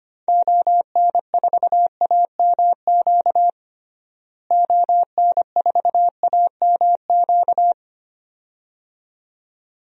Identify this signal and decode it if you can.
Morse code, heard as a keyed tone, about 25 wpm, 700 Hz. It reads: ON4AMQ ON4AMQ
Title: DXCC-100 Call Signs - 6 Characters - Repeated 2x - 25wpm